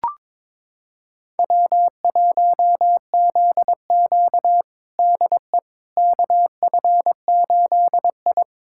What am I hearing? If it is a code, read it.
Morse code, heard as a keyed tone, about 21 wpm, 700 Hz. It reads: W1ZQ DE KF8I
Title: USA Callsign DE Callsign - 4 Characters - 22wpm